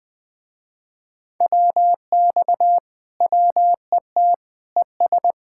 Morse code, heard as a keyed tone, about 20 wpm, 700 Hz. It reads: WX WET ES
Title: Speed-Racing Common QSO Phrases Encore - 20wpm